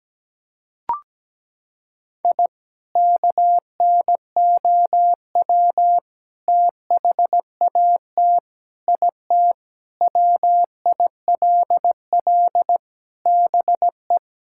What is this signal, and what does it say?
Morse code, heard as a keyed tone, about 17 wpm, 700 Hz. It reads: I KNOW THAT IT WILL BE
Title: Sentences from Top 1000 Words - Review Part 6 - 17wpm